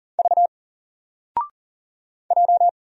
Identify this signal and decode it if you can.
Morse code, heard as a keyed tone, about 40 wpm, 700 Hz. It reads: V J
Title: Single Letter-Number 40wpm